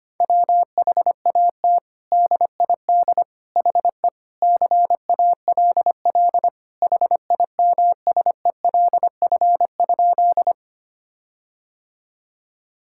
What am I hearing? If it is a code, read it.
Morse code, heard as a keyed tone, about 25 wpm, 700 Hz. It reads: WHAT DID HE CALL HIMSELF?
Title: Sentences from Top 1600 Words 25wpm